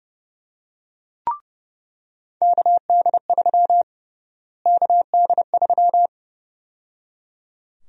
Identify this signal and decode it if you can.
Morse code, heard as a keyed tone, about 32 wpm, 700 Hz. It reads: KD3 KD3
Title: USA Call Sign Prefixes - Repeated 2x - 30wpm